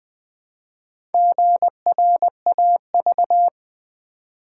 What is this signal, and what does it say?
Morse code, heard as a keyed tone, about 20 wpm, 700 Hz. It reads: GRAV
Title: Build A Word - 1 Letter At A Time - 3 to 5-Letter Words - 20wpm